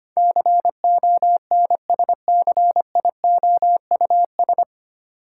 Morse code, heard as a keyed tone, about 25 wpm, 700 Hz. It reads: CONSCIOUS
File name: Nine Letter Words - 25wpm